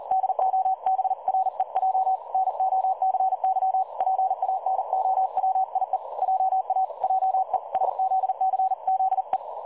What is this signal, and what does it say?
Morse code, heard as a keyed tone, about 39 wpm, 760 Hz. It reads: DOBREJ NOCY PANOWIE JA JESZCZE
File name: dlazaawansowanych / nagraniazpasma / websdr recording 3565.0kHz